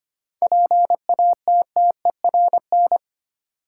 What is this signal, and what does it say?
Morse code, heard as a keyed tone, about 25 wpm, 700 Hz. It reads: PATTERN
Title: Seven Letter Words - 25wpm